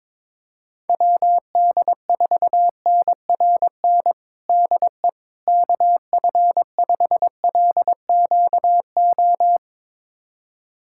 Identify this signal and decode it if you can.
Morse code, heard as a keyed tone, about 22 wpm, 700 Hz. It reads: WD4NRN DE KF5LQO
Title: USA Callsign DE Callsign - 6 Characters - 22wpm